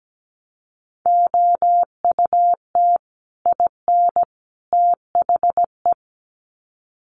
Morse code, heard as a keyed tone, about 17 wpm, 700 Hz. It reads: OUT IN THE
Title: Sets of 3 Words 17wpm